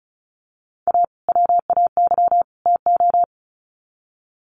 Morse code, heard as a keyed tone, about 34 wpm, 700 Hz. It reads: A WAY TO
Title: Sets of 3 Words 35wpm